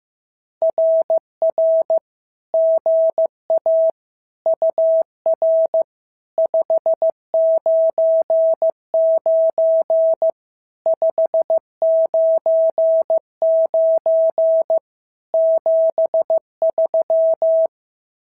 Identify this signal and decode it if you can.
Morse code, heard as a keyed tone, about 15 wpm, 650 Hz. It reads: RR GA UR 599 599 73
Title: SOTA QSO Practice 15wpm